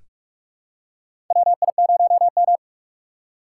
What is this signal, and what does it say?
Morse code, heard as a keyed tone, about 44 wpm, 700 Hz. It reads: WI0M
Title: USA Call Signs - 4 Characters - 6x Character Spacing - 45wpm